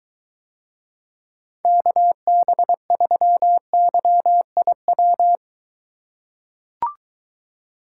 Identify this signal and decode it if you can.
Morse code, heard as a keyed tone, about 23 wpm, 700 Hz. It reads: KB3YIW